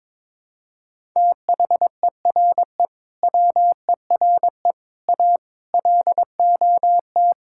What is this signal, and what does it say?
Morse code, heard as a keyed tone, about 23 wpm, 700 Hz. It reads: THERE WERE A LOT
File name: practice / 22wpm / Build A Phrase - 5-Word Phrases - 22wpm